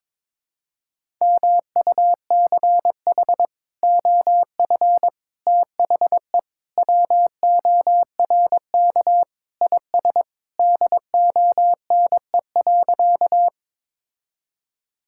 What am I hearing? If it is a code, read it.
Morse code, heard as a keyed tone, about 21 wpm, 700 Hz. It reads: MUCH OF THE WORK IS DONE.
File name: practice / 22wpm / Sentences with 3-Word Phrases - 22wpm